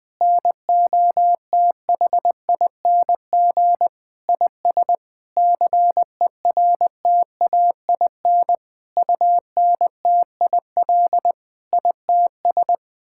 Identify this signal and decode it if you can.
Morse code, heard as a keyed tone, about 20 wpm, 700 Hz. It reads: NOTHING IS CERTAIN UNTIL ITS
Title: Sets of 8 Words - 20wpm